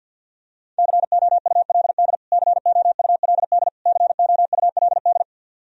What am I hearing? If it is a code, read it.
Morse code, heard as a keyed tone, about 54 wpm, 700 Hz. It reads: COULD COULD COULD
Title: Top 100 Words Repeated 50wpm